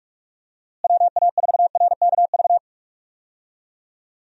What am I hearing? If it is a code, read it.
Morse code, heard as a keyed tone, about 46 wpm, 700 Hz. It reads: WA4RKV
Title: USA Call Signs - 6 Characters - Repeated 3x - 45wpm